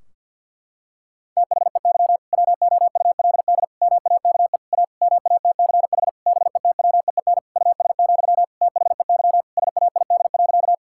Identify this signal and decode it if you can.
Morse code, heard as a keyed tone, about 50 wpm, 700 Hz. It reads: THEY WOULD MAKE A MATCH BETWEEN US, THEY SAID.
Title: Sentences from Top 1600 Words 50wpm